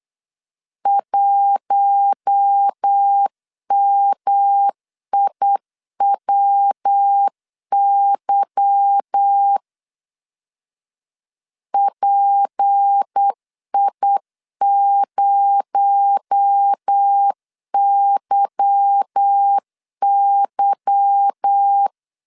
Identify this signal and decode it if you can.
Morse code, heard as a keyed tone, about 8 wpm, 800 Hz. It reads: 1MIWY PI0YY
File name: LEZIONI / 8 YJ45 / 8 Y 8wpm 3245